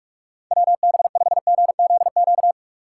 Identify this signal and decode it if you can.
Morse code, heard as a keyed tone, about 42 wpm, 690 Hz. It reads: WB5GZQ
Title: USA Call Signs - 6 Characters - 2x Character Spacing - 45wpm